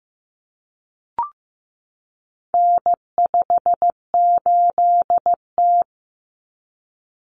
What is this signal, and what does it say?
Morse code, heard as a keyed tone, about 15 wpm, 700 Hz. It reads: N58T
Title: Common Cut Numbers - 4-Digit 15wpm